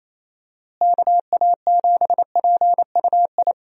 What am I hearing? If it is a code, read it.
Morse code, heard as a keyed tone, about 28 wpm, 700 Hz. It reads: KA7PUI
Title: USA Call Signs - 6 Characters - Repeated 3x - 28wpm